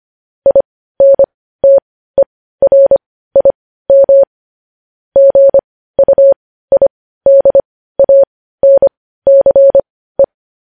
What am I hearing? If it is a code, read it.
Morse code, heard as a keyed tone, about 25 wpm, spaced as for 15 wpm, 550 Hz. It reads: INTERIM GUIDANCE
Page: news-headlines-in-morse-code-at-15-wpm